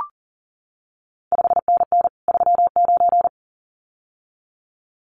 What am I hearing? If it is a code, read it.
Morse code, heard as a keyed tone, about 38 wpm, 700 Hz. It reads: 5NN 39
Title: CQ WW DX Contest Exchange Encore - 40wpm